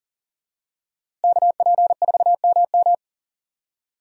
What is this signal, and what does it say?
Morse code, heard as a keyed tone, about 40 wpm, 700 Hz. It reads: KP4MM
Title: USA Call Signs - 5 Characters - Repeated 2x - 40wpm